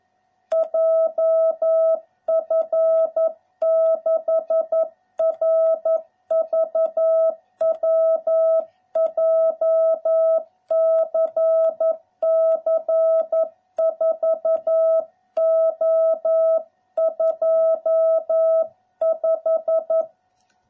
Morse code, heard as a keyed tone, about 11 wpm, 660 Hz. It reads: JF6RVW JCC 4O25